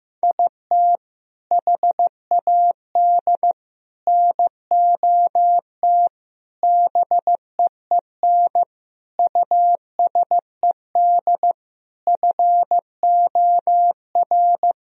Morse code, heard as a keyed tone, about 15 wpm, 700 Hz. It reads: IT HAD NOT BEEN USED FOR
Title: Sentences from Top 1000 Words - Review Part 1 - 15wpm